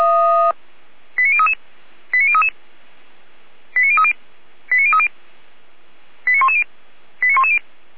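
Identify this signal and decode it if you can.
Five-tone selective call (ZVEI), heard as quick runs of five tones, tones of about 70 ms. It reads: 89033 89033 89030 89030 89110 89110
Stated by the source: ZVEI five-tone selective call, code 89508, received via SDR.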